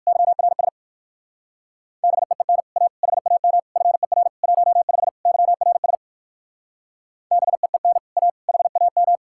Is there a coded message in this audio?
Yes: Morse code from 0.1 to 9.2 s, heard as a keyed tone, about 59 wpm, 700 Hz. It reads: YRS BEEN A HAM FER 15 YRS BEEN A HAM